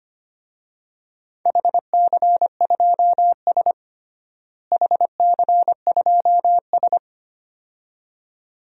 Morse code, heard as a keyed tone, about 25 wpm, 700 Hz. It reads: HC2S HC2S
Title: DXCC-100 Call Signs - 4 Characters - Repeated 2x - 25wpm